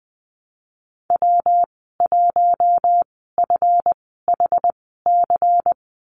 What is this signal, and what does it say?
Morse code, heard as a keyed tone, about 20 wpm, spaced as for 15 wpm, 700 Hz. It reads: W1FHC
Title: 179 Lesson 29g - Slash - Call Signs 20wpm